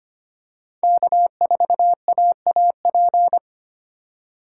Morse code, heard as a keyed tone, about 25 wpm, 700 Hz. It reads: K4AAP
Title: USA Call Signs - 5 Characters - 3x Character Spacing - 25wpm